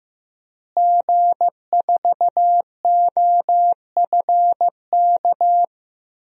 Morse code, heard as a keyed tone, about 15 wpm, 700 Hz. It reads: G4OFK
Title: DXCC-100 Call Signs - 5 Characters - Repeated 3x - 15wpm